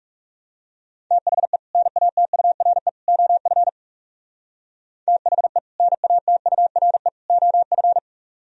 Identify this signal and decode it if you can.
Morse code, heard as a keyed tone, about 37 wpm, 700 Hz. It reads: THE NATURE OF THE NATURE OF
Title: Speed-Racing - Sets of 3 Words 30wpm